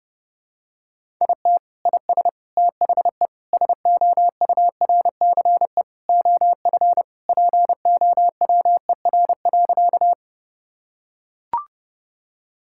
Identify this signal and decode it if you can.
Morse code, heard as a keyed tone, about 30 wpm, 700 Hz. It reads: IT IS THE SOURCE OF POWER.
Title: Sentences from Top 1000 Words - Review Part 5 - 30wpm